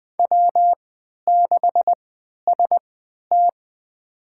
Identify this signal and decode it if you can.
Morse code, heard as a keyed tone, about 21 wpm, spaced as for 12 wpm, 700 Hz. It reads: W6ST
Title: USA Call Signs - 4 Characters - 3x Character Spacing - 20wpm